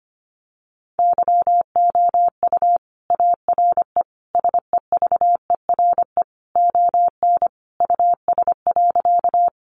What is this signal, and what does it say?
Morse code, heard as a keyed tone, about 25 wpm, 700 Hz. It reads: YOU ARE SEVERE ON US.
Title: Sentences from Top 1400 Words 25wpm